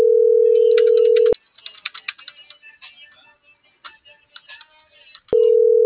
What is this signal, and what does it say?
Ringback tone (North America), 440 + 480 Hz.